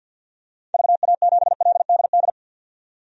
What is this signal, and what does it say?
Morse code, heard as a keyed tone, about 51 wpm, 700 Hz. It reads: VA7LDD
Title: DXCC-100 Call Signs - 6 Characters - 50wpm